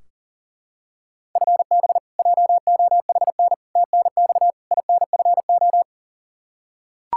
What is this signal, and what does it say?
Morse code, heard as a keyed tone, about 37 wpm, 700 Hz. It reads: FB JOHN TNX INFO
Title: Practice QSO - Exchange 3 - More Info - 40wpm